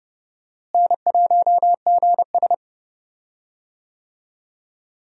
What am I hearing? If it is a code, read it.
Morse code, heard as a keyed tone, about 30 wpm, 700 Hz. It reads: N1GS
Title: USA Call Signs - 4 Characters - Repeated 2x - 30wpm